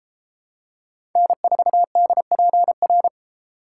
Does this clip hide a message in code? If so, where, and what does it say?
Yes: Morse code from 1.2 to 3.1 s, heard as a keyed tone, about 30 wpm, 700 Hz. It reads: N4DPR